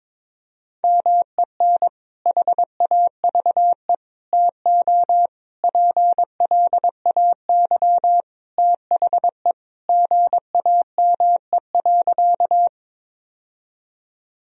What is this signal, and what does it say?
Morse code, heard as a keyed tone, about 21 wpm, 700 Hz. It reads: MEN HAVE TO PLAY THE GAME.
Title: Sentences from Top 1000 Words - Review Part 3 - 22wpm